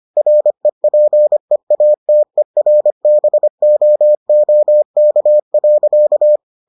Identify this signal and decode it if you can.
Morse code, heard as a keyed tone, about 25 wpm, 600 Hz. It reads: REPEATERBOOK.